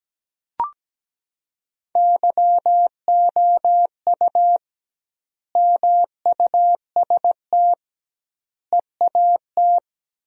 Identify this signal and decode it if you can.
Morse code, heard as a keyed tone, about 17 wpm, 700 Hz. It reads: YOU MUST EAT
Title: 2x Word-Spacing - Sentences from Top 300 Words - 17wpm